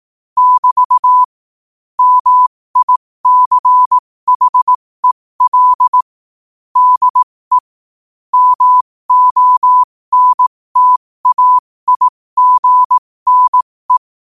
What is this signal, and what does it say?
Morse code, heard as a keyed tone, about 18 wpm, spaced as for 15 wpm, 1000 Hz. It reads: <BT> MICHEL DE MONTAIGNE
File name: qotdfiles / 15wpm